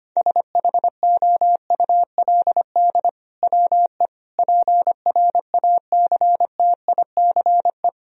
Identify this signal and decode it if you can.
Morse code, heard as a keyed tone, about 25 wpm, 700 Hz. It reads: SHOULD WE PRACTICE
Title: Sentences from Top 1000 Words - Review Part 4 - 25wpm